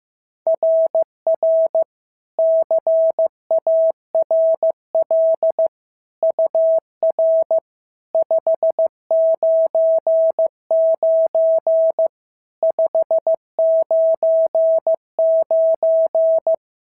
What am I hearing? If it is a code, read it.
Morse code, heard as a keyed tone, about 14 wpm, 650 Hz. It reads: RR CARL UR 599 599
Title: SOTA QSO Practice 15wpm